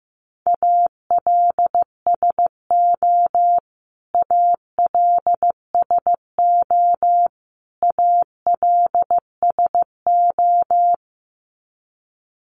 Morse code, heard as a keyed tone, about 15 wpm, 700 Hz. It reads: ALSO ALSO ALSO
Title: Top 500 Words Repeated 15wpm